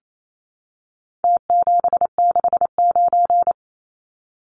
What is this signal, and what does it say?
Morse code, heard as a keyed tone, about 28 wpm, 700 Hz. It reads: T769
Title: Numbers - Contest Style - 4-Digit 28wpm